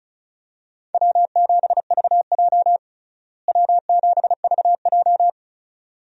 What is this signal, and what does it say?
Morse code, heard as a keyed tone, about 36 wpm, 700 Hz. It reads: W7VJ W7VJ
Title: USA Call Signs - 4 Characters - Repeated 2x - 35wpm